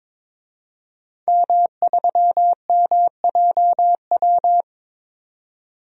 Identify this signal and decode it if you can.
Morse code, heard as a keyed tone, about 21 wpm, 700 Hz. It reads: M3MJW
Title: DXCC-100 Call Signs - 5 Characters - 22wpm